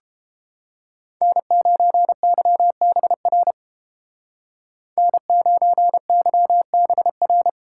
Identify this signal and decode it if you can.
Morse code, heard as a keyed tone, about 32 wpm, 700 Hz. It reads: N9YBR N9YBR